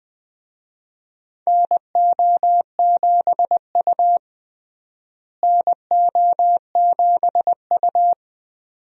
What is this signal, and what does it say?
Morse code, heard as a keyed tone, about 20 wpm, 700 Hz. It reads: NO7U NO7U
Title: USA Call Signs - 4 Characters - Repeated 2x - 20wpm